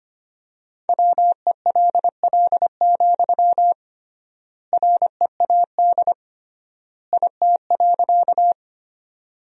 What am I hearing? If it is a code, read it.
Morse code, heard as a keyed tone, about 25 wpm, 700 Hz. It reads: WELL, READ IT.
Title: 3x Word-Spacing - Sentences from Top 200 Words - 25wpm